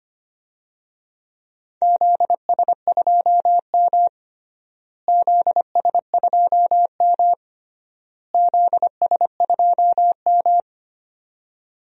Morse code, heard as a keyed tone, about 25 wpm, 700 Hz. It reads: ZS2M ZS2M ZS2M
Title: DXCC-100 Call Signs - 4 Characters - Repeated 3x - 25wpm